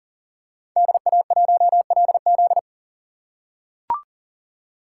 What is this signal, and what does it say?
Morse code, heard as a keyed tone, about 40 wpm, 700 Hz. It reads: DA1LZ
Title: USA Call Signs - 3x Character Spacing - 40wpm